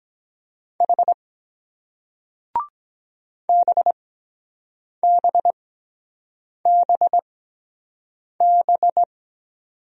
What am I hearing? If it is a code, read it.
Morse code, heard as a keyed tone, about 23 wpm, spaced as for 7 wpm, 700 Hz. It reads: H BBBB